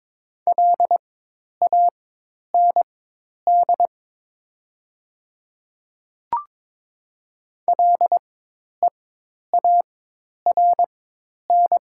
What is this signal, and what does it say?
Morse code, heard as a keyed tone, about 22 wpm, spaced as for 10 wpm, 700 Hz. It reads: LAND LEARN
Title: 4x Character Spacing - Top 300 Words - 22wpm